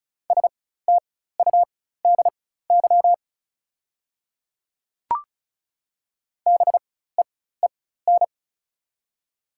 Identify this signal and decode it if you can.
Morse code, heard as a keyed tone, about 36 wpm, spaced as for 16 wpm, 690 Hz. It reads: STUDY BEEN
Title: 4x Character Spacing - Top 300 Words - 35wpm